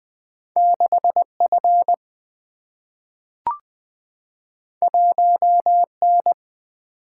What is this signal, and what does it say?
Morse code, heard as a keyed tone, about 20 wpm, 700 Hz. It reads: 6F 1N
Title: Single-Letter Single-Number 20wpm